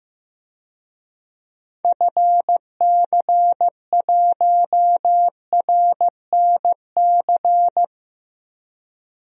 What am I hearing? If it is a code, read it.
Morse code, heard as a keyed tone, about 15 wpm, 700 Hz. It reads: FC1RNC
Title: DXCC-100 Call Signs - 6 Characters - Repeated 3x - 15wpm